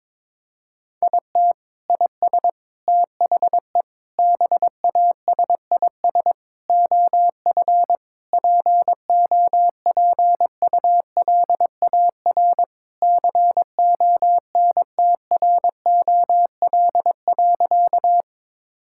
Morse code, heard as a keyed tone, about 22 wpm, 700 Hz. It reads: IT IS THE BASIS OF POPULAR CONTROL.
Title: Sentences from Top 1000 Words - Review Part 5 - 22wpm